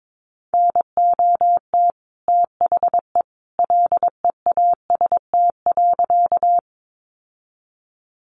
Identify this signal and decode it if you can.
Morse code, heard as a keyed tone, about 22 wpm, 700 Hz. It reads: NOT THE LEAST.
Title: Sentences from Top 700 Words 22wpm